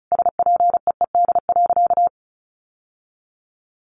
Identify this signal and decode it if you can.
Morse code, heard as a keyed tone, about 34 wpm, 700 Hz. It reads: SPEED.
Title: Sentences from Top 1100 Words 35wpm